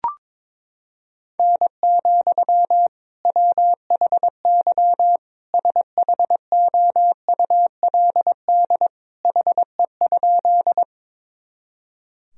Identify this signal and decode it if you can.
Morse code, heard as a keyed tone, about 22 wpm, 700 Hz. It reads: N, WHY SHOULD HE?
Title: Sentences from Top 300 Words and Abbreviations 22wpm